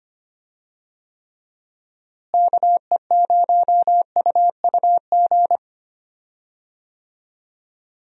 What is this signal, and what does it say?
Morse code, heard as a keyed tone, about 25 wpm, 700 Hz. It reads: KE0UUG
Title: USA Call Signs - 6 Characters - Repeated 2x - 25wpm